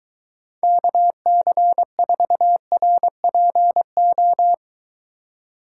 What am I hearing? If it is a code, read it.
Morse code, heard as a keyed tone, about 23 wpm, 700 Hz. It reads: KC4RPO